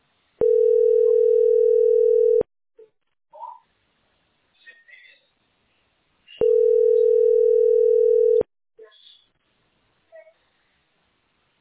Ringback tone (North America), 440 + 480 Hz.
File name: e6fc4ff0-6417-4123-89eb